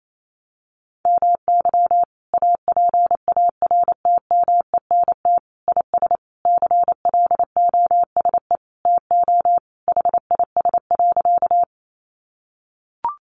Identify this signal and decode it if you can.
Morse code, heard as a keyed tone, about 28 wpm, 700 Hz. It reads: MY APARTMENT IS CLOSE TO HIS.
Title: Sentences from Top 900 Words 28wpm